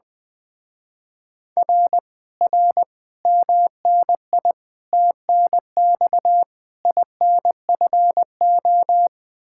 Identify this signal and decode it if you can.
Morse code, heard as a keyed tone, about 21 wpm, 700 Hz. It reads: R R MNI TNX INFO
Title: Speed-Racing Common QSO Phrases - 20wpm